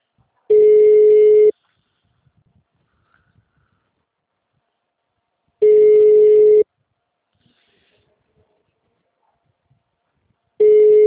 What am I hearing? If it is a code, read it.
Ringback tone (Europe, 425 Hz).